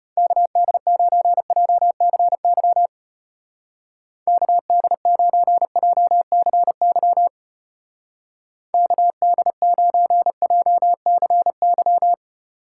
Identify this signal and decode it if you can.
Morse code, heard as a keyed tone, about 31 wpm, 700 Hz. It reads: KD9JCY KD9JCY KD9JCY